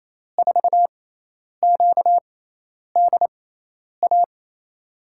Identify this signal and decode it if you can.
Morse code, heard as a keyed tone, about 28 wpm, spaced as for 10 wpm, 700 Hz. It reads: 4QDA
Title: USA Call Signs - 6x Character Spacing - 28wpm